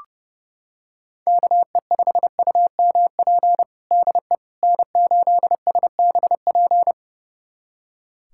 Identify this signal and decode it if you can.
Morse code, heard as a keyed tone, about 31 wpm, 700 Hz. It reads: KE5UMP DE N8SBP
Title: USA Callsign DE Callsign 30wpm